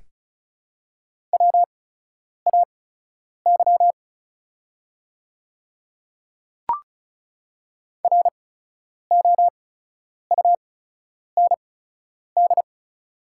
Morse code, heard as a keyed tone, about 34 wpm, spaced as for 10 wpm, 700 Hz. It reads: WAY ROUND
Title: 8x Character Spacing - Top 300 Words - 35wpm